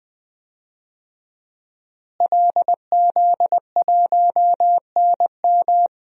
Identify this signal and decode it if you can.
Morse code, heard as a keyed tone, about 20 wpm, 700 Hz. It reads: LZ1NM
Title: DXCC-100 Call Signs - 5 Characters - Repeated 3x - 20wpm